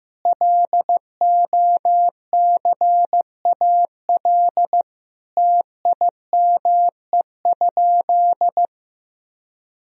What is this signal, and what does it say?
Morse code, heard as a keyed tone, about 16 wpm, 700 Hz. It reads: LOCAL TIME?
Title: Sentences from Top 1000 Words - Review Part 3 - 15wpm